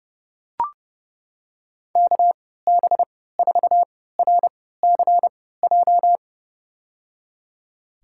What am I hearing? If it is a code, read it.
Morse code, heard as a keyed tone, about 32 wpm, spaced as for 17 wpm, 700 Hz. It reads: KB4RCJ
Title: USA Call Signs - 6 Characters - 3x Character Spacing - 30wpm